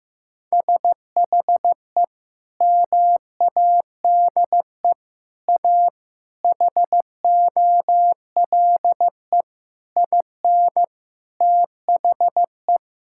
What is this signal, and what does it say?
Morse code, heard as a keyed tone, about 15 wpm, 700 Hz. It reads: SHE MADE A HOLE IN THE
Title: Sentences from Top 1600 Words 15wpm